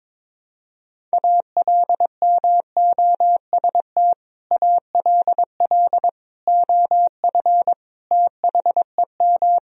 Morse code, heard as a keyed tone, about 21 wpm, 700 Hz. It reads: ALMOST ALL OF THEM
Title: Sentences from Top 1000 Words - Review Part 4 - 22wpm